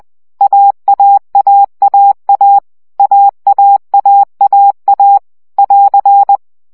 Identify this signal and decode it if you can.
Morse code, heard as a keyed tone, about 20 wpm, 800 Hz. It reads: AAAAA AAAAA <AR>
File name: CW gyakorlas a 17 baud